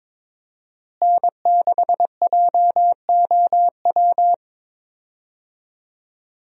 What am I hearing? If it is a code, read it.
Morse code, heard as a keyed tone, about 21 wpm, 700 Hz. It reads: N6JOW
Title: USA Call Signs - 5 Characters - Repeated 2x - 22wpm